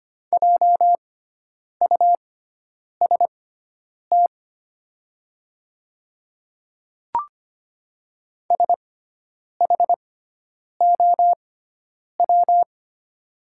Morse code, heard as a keyed tone, about 25 wpm, spaced as for 9 wpm, 700 Hz. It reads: JUST SHOW
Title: 6x Character Spacing - Top 300 Words - 25wpm